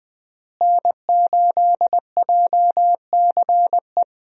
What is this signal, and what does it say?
Morse code, heard as a keyed tone, about 20 wpm, 700 Hz. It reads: N8JCE
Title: USA Call Signs - 5 Characters - Repeated 3x - 20wpm